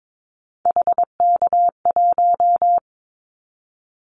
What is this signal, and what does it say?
Morse code, heard as a keyed tone, about 22 wpm, 700 Hz. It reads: HK1
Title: Callsign Prefixes for Top 100 DXCC - 22wpm